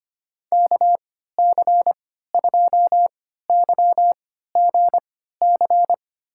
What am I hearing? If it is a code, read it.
Morse code, heard as a keyed tone, about 25 wpm, spaced as for 14 wpm, 700 Hz. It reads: KC2YGC
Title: USA Call Signs - 6 Characters - 3x Character Spacing - 25wpm